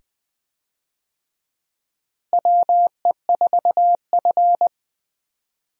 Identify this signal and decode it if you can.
Morse code, heard as a keyed tone, about 20 wpm, 700 Hz. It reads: WE4F
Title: USA Call Signs - 4 Characters - Repeated 2x - 20wpm